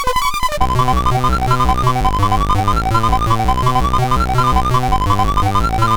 There is no code.